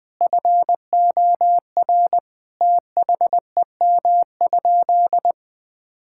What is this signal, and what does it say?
Morse code, heard as a keyed tone, about 20 wpm, 700 Hz. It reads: FOR THEM?
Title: Sentences from Top 1000 Words - Review Part 5 - 20wpm